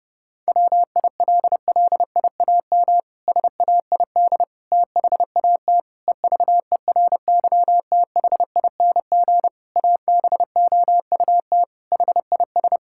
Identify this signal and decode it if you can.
Morse code, heard as a keyed tone, about 30 wpm, 700 Hz. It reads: WILLIAM SAID THAT EVERYTHING ABOUT HIS
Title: Pangram Challenge v2 - 30wpm